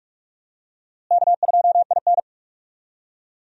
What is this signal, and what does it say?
Morse code, heard as a keyed tone, about 46 wpm, 700 Hz. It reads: K2IN
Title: USA Call Signs - 4 Characters - Repeated 2x - 45wpm